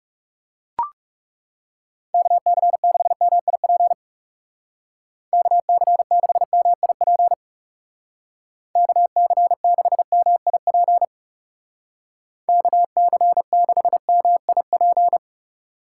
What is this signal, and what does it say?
Morse code, heard as a keyed tone, about 37 wpm, 700 Hz. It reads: KC6MIP KC6MIP KC6MIP KC6MIP
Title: Speed Racing - USA 2x3 Call Signs 30wpm